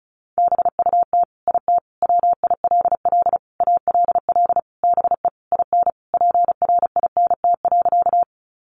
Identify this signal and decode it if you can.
Morse code, heard as a keyed tone, about 34 wpm, 700 Hz. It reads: BUT IT WILL ALL BE IN PRINT.
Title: Sentences from Top 1400 Words 35wpm